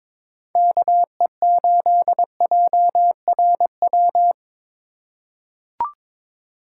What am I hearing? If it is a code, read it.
Morse code, heard as a keyed tone, about 22 wpm, 700 Hz. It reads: KE8JRW
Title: USA Call Signs - 6 Characters - 4x Character Spacing - 22wpm